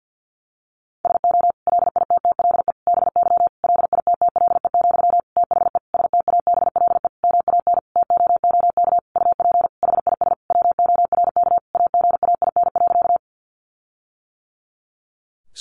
Morse code, heard as a keyed tone, about 50 wpm, 700 Hz. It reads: SO LITTLE BY LITTLE, THE STABLE MAN TOOK UP HIS WORK AGAIN.
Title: Sentences from Top 1500 Words 50wpm